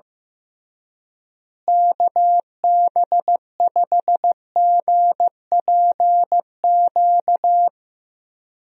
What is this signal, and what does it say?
Morse code, heard as a keyed tone, about 15 wpm, 700 Hz. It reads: KB5GPQ
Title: USA Call Signs - 6 Characters - Repeated 3x - 15wpm